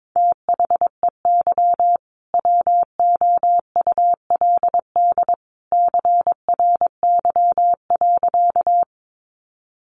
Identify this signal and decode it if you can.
Morse code, heard as a keyed tone, about 22 wpm, 700 Hz. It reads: THEY WOULD CRY.
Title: Sentences from Top 500 Words 22wpm